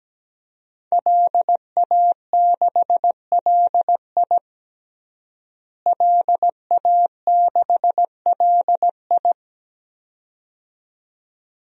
Morse code, heard as a keyed tone, about 17 wpm, 700 Hz. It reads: LA6LI LA6LI
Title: DXCC-100 Call Signs - 5 Characters - Repeated 2x - 17wpm